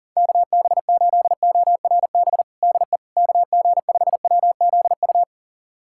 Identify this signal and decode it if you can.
Morse code, heard as a keyed tone, about 40 wpm, 700 Hz. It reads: KB8ORB DE KG5WZU
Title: USA Callsign DE Callsign - 6 Characters - 40wpm